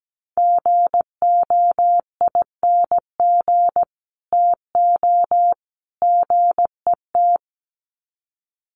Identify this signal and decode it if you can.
Morse code, heard as a keyed tone, about 17 wpm, 700 Hz. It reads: GOING TO GET
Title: Sets of 3 Words 17wpm